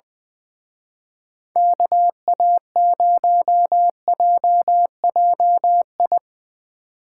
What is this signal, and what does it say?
Morse code, heard as a keyed tone, about 23 wpm, 700 Hz. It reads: KA0JJI
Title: Speed Racing - USA Call Signs 20wpm